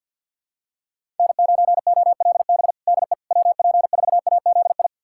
Morse code, heard as a keyed tone, about 52 wpm, 700 Hz. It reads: N9OLB DE WP4AZI
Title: USA Callsign DE Callsign 50wpm